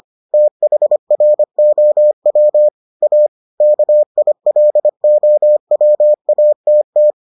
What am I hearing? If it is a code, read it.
Morse code, heard as a keyed tone, about 25 wpm, 600 Hz. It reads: THROW A KILOWATT